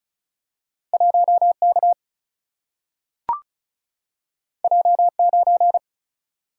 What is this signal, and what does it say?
Morse code, heard as a keyed tone, about 36 wpm, 700 Hz. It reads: 1 K J 9
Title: Single-Letter Single-Number 35wpm